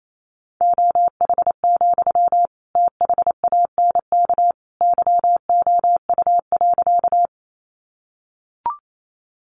Morse code, heard as a keyed tone, about 28 wpm, 700 Hz. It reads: OH, THANK YOU.
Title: Sentences from Top 600 Words 28wpm